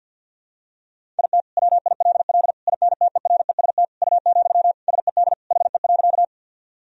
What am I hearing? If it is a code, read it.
Morse code, heard as a keyed tone, about 50 wpm, 700 Hz. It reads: IT WILL INTEREST U, SED HE.
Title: Sentences from Top 400 Words and Abbreviations 50wpm